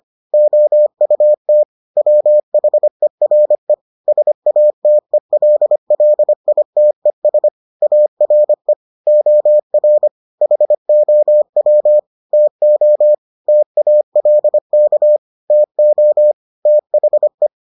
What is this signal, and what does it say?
Morse code, heard as a keyed tone, about 25 wpm, 600 Hz. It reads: OUT WHERE SATELLITES ARE OR HOW TO TALK TO THE